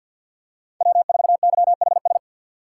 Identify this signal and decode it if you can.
Morse code, heard as a keyed tone, about 51 wpm, 700 Hz. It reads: W4YHS
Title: USA Call Signs - 5 Characters - 2x Character Spacing - 50wpm